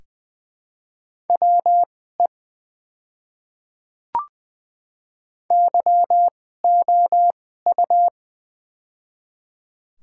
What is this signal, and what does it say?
Morse code, heard as a keyed tone, about 20 wpm, spaced as for 15 wpm, 700 Hz. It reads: WE YOU
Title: 2x Character Spacing - Top 200 Words - 20wpm